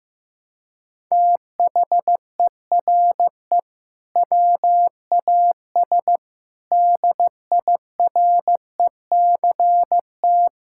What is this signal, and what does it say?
Morse code, heard as a keyed tone, about 16 wpm, 700 Hz. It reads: THERE WAS DIRECT
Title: Sentences from Top 1000 Words - Review Part 6 - 15wpm